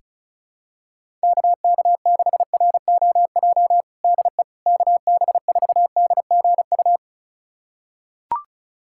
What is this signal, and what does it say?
Morse code, heard as a keyed tone, about 36 wpm, 700 Hz. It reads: KK6ROJ DE KB4DGU
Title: USA Callsign DE Callsign - 6 Characters - 35wpm